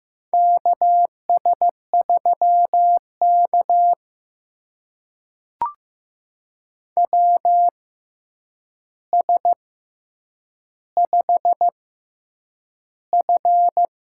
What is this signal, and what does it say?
Morse code, heard as a keyed tone, about 15 wpm, spaced as for 8 wpm, 700 Hz. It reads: KS3K WS5F
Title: USA Call Signs - 4 Characters - 6x Character Spacing - 15wpm